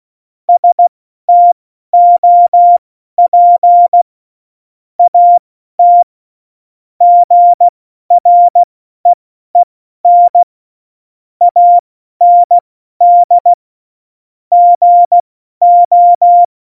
Morse code, heard as a keyed tone, about 16 wpm, spaced as for 12 wpm, 700 Hz. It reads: STOP AT GREEN AND GO
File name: Morsekurs / Witze